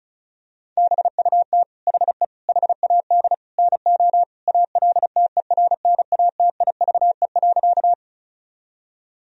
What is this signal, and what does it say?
Morse code, heard as a keyed tone, about 36 wpm, 700 Hz. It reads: BUT HE HAD NO ALTERNATIVE.
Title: Sentences from Top 1000 Words - Review Part 4 - 35wpm